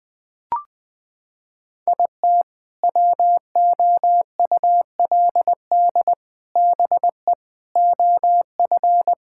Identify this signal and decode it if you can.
Morse code, heard as a keyed tone, about 20 wpm, 700 Hz. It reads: IT WOULD BE OF
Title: Sentences from Top 1000 Words - Review Part 6 - 20wpm